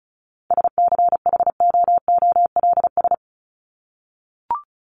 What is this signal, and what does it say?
Morse code, heard as a keyed tone, about 34 wpm, 700 Hz. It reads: SCHOOLS
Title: ES:S Suffix - Top 100 Words with Base Words 35wpm